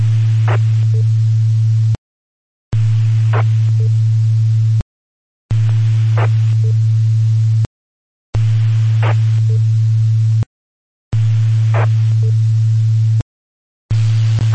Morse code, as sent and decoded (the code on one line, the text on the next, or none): none